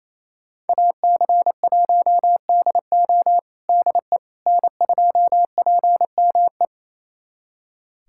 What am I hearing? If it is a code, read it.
Morse code, heard as a keyed tone, about 28 wpm, 700 Hz. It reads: AC1DO DE N2PME
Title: USA Callsign DE Callsign - 5 Characters - 28wpm